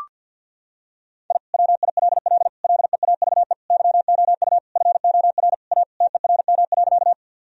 Morse code, heard as a keyed tone, about 50 wpm, 700 Hz. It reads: I WILL LEAVE YOU FOR A TERM.
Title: Sentences from Top 1000 Words - Review Part 4 - 50wpm